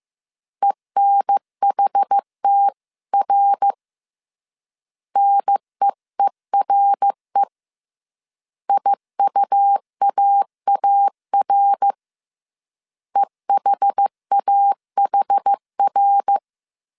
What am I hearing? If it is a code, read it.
Morse code, heard as a keyed tone, about 15 wpm, 800 Hz. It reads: ENHTR NEERE IUAAR EHAHR